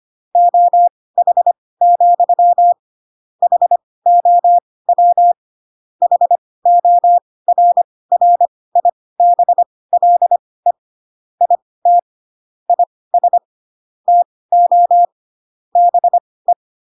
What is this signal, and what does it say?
Morse code, heard as a keyed tone, about 25 wpm, spaced as for 18 wpm, 700 Hz. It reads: OH, HOW HORRIBLE IT IS TO BE